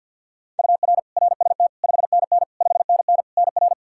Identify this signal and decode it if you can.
Morse code, heard as a keyed tone, about 52 wpm, 700 Hz. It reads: UR RST 5NN 5NN NR
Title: Practice QSO - Exchange 1 - Beginning - 50wpm